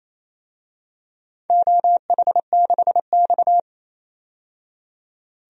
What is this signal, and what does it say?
Morse code, heard as a keyed tone, about 28 wpm, 700 Hz. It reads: OH6X
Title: DXCC-100 Call Signs - 4 Characters - 28wpm